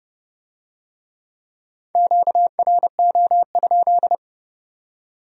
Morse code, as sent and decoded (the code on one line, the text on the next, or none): --.- .-. --- ..--..
QRO?